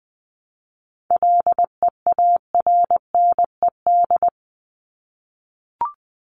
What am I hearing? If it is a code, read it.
Morse code, heard as a keyed tone, about 20 wpm, 700 Hz. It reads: LEARNED
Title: ED Suffix - Top 100 Words with Base Words 20wpm